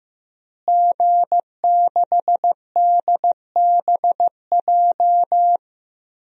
Morse code, heard as a keyed tone, about 15 wpm, 700 Hz. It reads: G6DBJ
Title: DXCC-100 Call Signs - 5 Characters - Repeated 3x - 15wpm